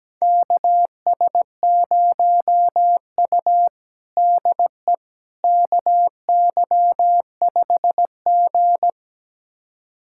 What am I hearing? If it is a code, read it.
Morse code, heard as a keyed tone, about 17 wpm, 700 Hz. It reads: KS0U DE KY5G
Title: USA Callsign DE Callsign - 4 Characters - 17wpm